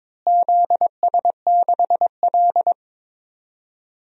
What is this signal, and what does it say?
Morse code, heard as a keyed tone, about 22 wpm, 700 Hz. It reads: ZS6L
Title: DXCC-100 Call Signs - 4 Characters - Repeated 2x - 22wpm